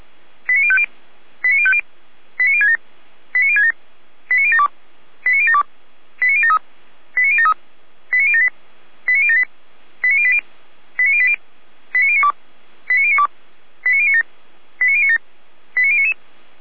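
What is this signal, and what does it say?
Five-tone selective call (ZVEI), heard as quick runs of five tones, tones of about 70 ms. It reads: 89050 89050 89067 89067 89072 89072 89073 89073 89078 89078 89080 89080 89092 89092 89097 89097 89099
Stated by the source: ZVEI five-tone selective call received on an SDR.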